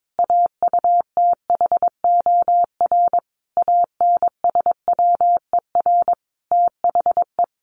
Morse code, heard as a keyed tone, about 22 wpm, 700 Hz. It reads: AUTHOR ANSWER THE
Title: Sentences from Top 1200 Words 22wpm